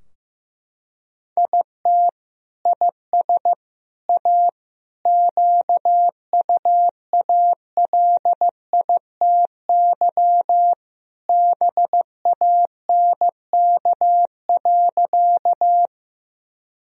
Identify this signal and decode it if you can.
Morse code, heard as a keyed tone, about 16 wpm, 700 Hz. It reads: IT IS A QUALITY BANK.
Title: Sentences from Top 1000 Words - Review Part 4 - 15wpm